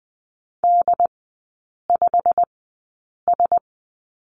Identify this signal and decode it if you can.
Morse code, heard as a keyed tone, about 20 wpm, spaced as for 8 wpm, 700 Hz. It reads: D5S
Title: 107 Lesson 18e - Letter m - Three Characters 20wpm